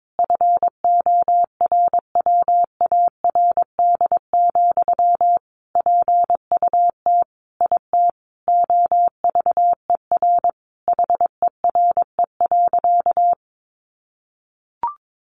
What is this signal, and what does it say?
Morse code, heard as a keyed tone, about 22 wpm, 700 Hz. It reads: FORWARD, PUT IT OVER HERE.
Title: Sentences from Top 1500 Words 22wpm